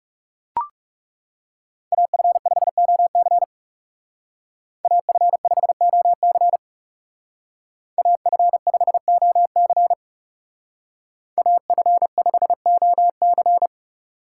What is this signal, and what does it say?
Morse code, heard as a keyed tone, about 34 wpm, 700 Hz. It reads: AF5OC AF5OC AF5OC AF5OC
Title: Speed Racing - USA 2x2 Call Signs 30wpm